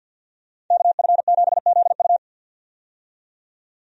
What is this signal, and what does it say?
Morse code, heard as a keyed tone, about 51 wpm, 700 Hz. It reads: KF7ZU
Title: USA Call Signs - 5 Characters - 50wpm